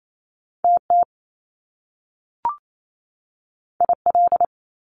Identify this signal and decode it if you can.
Morse code, heard as a keyed tone, about 28 wpm, 700 Hz. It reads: TT IL
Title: Top Two Letter Combinations in Words 28wpm